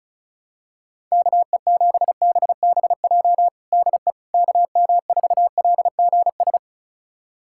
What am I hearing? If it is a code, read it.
Morse code, heard as a keyed tone, about 36 wpm, 700 Hz. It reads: KE7BBJ DE KM4LGS
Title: USA Callsign DE Callsign 35wpm